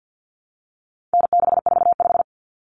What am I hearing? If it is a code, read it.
Morse code, heard as a keyed tone, about 50 wpm, 700 Hz. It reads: N645
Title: Common Cut Numbers - 4-Digit 50wpm